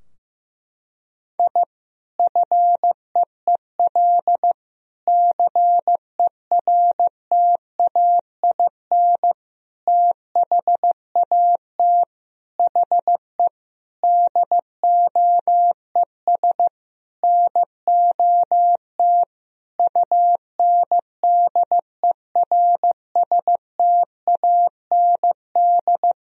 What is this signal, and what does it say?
Morse code, heard as a keyed tone, about 15 wpm, 700 Hz. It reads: I FEEL CERTAIN THAT HE DOES NOT UNDERSTAND
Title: Sentences from Top 1000 Words - Review Part 2 - 15wpm